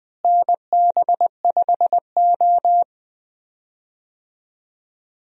Morse code, heard as a keyed tone, about 21 wpm, 700 Hz. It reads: NB5O
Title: USA Call Signs - 4 Characters - Repeated 3x - 20wpm